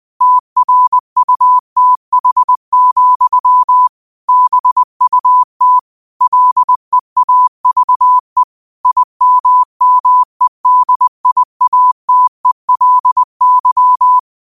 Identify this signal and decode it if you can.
Morse code, heard as a keyed tone, about 20 wpm, 1000 Hz. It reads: TRUTH, BUT LEAVE IMMEDIATELY
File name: qotdfiles / 20wpm